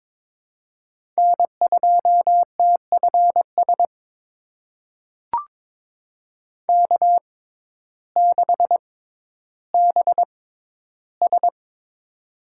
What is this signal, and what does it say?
Morse code, heard as a keyed tone, about 22 wpm, spaced as for 12 wpm, 700 Hz. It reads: N2TFS K6BS
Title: USA Call Signs - 6x Character Spacing - 22wpm